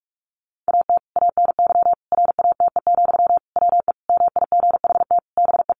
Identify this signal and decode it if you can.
Morse code, heard as a keyed tone, about 44 wpm, 700 Hz. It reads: AT ANY RATE, WE MIGHT BE
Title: Sentences from Top 700 Words 45wpm